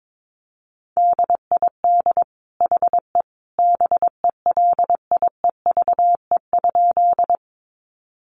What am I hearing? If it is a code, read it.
Morse code, heard as a keyed tone, about 22 wpm, 700 Hz. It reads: DID HE BELIEVE?
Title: Sentences from Top 1000 Words 22wpm